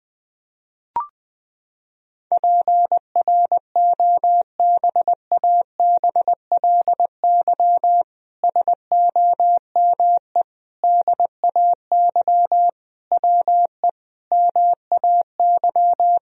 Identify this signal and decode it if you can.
Morse code, heard as a keyed tone, about 20 wpm, 700 Hz. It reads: PROBABLY SOME DAY WE MAY
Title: Sentences from Top 1000 Words - Review Part 4 - 20wpm